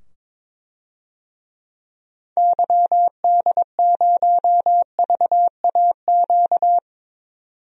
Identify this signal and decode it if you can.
Morse code, heard as a keyed tone, about 22 wpm, 700 Hz. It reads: YD0VAQ
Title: DXCC-100 Call Signs - 6 Characters - 22wpm